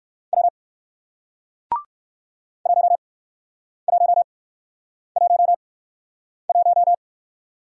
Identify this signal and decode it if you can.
Morse code, heard as a keyed tone, about 55 wpm, spaced as for 9 wpm, 700 Hz. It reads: W 1111
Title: Speed-Racing Single Letter-Number 45wpm